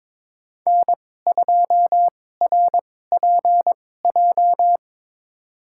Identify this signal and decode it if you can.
Morse code, heard as a keyed tone, about 22 wpm, spaced as for 16 wpm, 700 Hz. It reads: N2RPJ
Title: USA Call Signs - 5 Characters - 2x Character Spacing - 22wpm